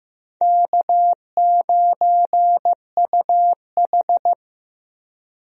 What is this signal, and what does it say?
Morse code, heard as a keyed tone, about 15 wpm, 700 Hz. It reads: K9UH
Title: USA Call Signs - 4 Characters - Repeated 2x - 15wpm